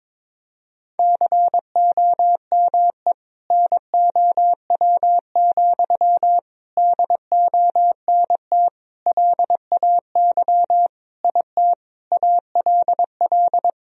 Morse code, heard as a keyed tone, about 22 wpm, 700 Hz. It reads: COME NOW, DONT LAY IT ALL
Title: Sentences from Top 1000 Words - Review Part 2 - 22wpm